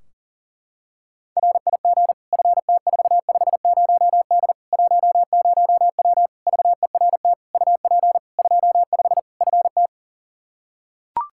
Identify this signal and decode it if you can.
Morse code, heard as a keyed tone, about 39 wpm, 700 Hz. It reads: RIG FT450D 10W VERT UP 25 FT
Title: Practice QSO - Exchange 2 - Info - 40wpm